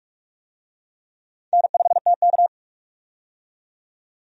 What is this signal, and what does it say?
Morse code, heard as a keyed tone, about 46 wpm, 700 Hz. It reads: N5TK
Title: USA Call Signs - 4 Characters - 45wpm